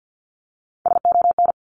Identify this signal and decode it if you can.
Morse code, heard as a keyed tone, about 50 wpm, 700 Hz. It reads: SON
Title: Top Three Letter Combinations in Words 50wpm